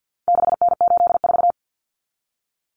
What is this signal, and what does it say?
Morse code, heard as a keyed tone, about 50 wpm, 700 Hz. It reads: 6N84
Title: Common Cut Numbers - 4-Digit 50wpm